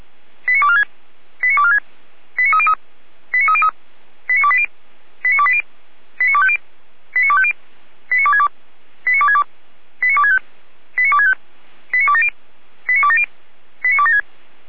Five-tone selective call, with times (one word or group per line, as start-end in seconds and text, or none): 0.5-0.8 89246
1.4-1.8 89246
2.4-2.7 89303
3.3-3.7 89303
4.3-4.7 89280
5.2-5.6 89280
6.2-6.6 89250
7.1-7.5 89250
8.1-8.5 89262
9.1-9.4 89262
10.0-10.4 89265
11.0-11.3 89265
11.9-12.3 89270
12.9-13.2 89270
13.8-14.2 89276